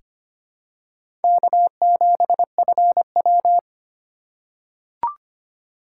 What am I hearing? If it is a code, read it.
Morse code, heard as a keyed tone, about 25 wpm, 700 Hz. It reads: K7FW
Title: USA Call Signs - 4 Characters - 4x Character Spacing - 25wpm